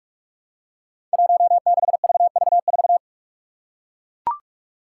Morse code, heard as a keyed tone, about 44 wpm, 700 Hz. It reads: 16VV4
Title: Five Challenging Letter-Number Combinations 45wpm